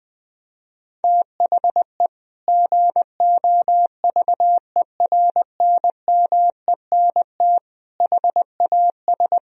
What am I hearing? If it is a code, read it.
Morse code, heard as a keyed tone, about 19 wpm, 700 Hz. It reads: THE GOVERNMENT HAS
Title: Sentences from Top 1000 Words - Review Part 3 - 20wpm